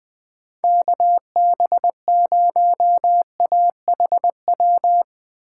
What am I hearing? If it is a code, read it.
Morse code, heard as a keyed tone, about 19 wpm, 700 Hz. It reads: KB0AHW
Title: Speed Racing - USA Call Signs 17wpm